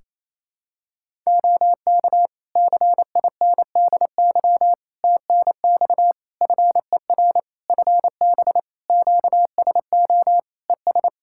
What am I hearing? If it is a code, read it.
Morse code, heard as a keyed tone, about 27 wpm, 700 Hz. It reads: OK CINDY TNX FER FB QSO ES
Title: K1USN SST QSO Practice 28wpm